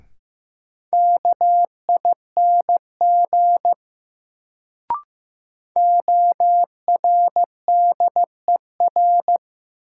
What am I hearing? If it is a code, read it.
Morse code, heard as a keyed tone, about 15 wpm, 700 Hz. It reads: KING ORDER
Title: Sentences from Top 400 Words - New Word Drill - 15wpm